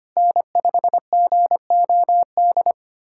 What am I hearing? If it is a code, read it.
Morse code, heard as a keyed tone, about 25 wpm, 700 Hz. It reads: N5GOD
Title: USA Callsign DE Callsign - 5 Characters - 25wpm